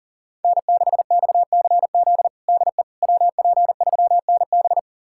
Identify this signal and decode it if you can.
Morse code, heard as a keyed tone, about 40 wpm, 700 Hz. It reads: N6XCZ DE WP3NB
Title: USA Callsign DE Callsign - 5 Characters - 40wpm